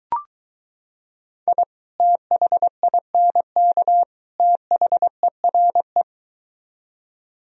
Morse code, heard as a keyed tone, about 23 wpm, 700 Hz. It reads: I THINK THERE